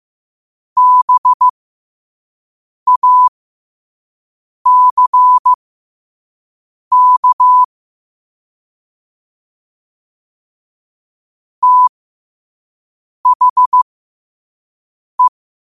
Morse code, heard as a keyed tone, about 15 wpm, spaced as for 5 wpm, 1000 Hz. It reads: BACK THE
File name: qotdfiles / 5wpm